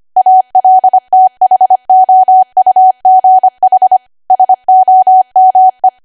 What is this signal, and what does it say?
Morse code, heard as a keyed tone, about 25 wpm, 750 Hz. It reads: ALTHOUGH SOME